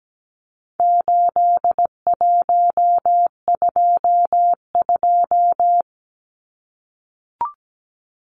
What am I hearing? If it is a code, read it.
Morse code, heard as a keyed tone, about 17 wpm, 700 Hz. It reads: 8122
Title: Numbers - 4-Digit 17wpm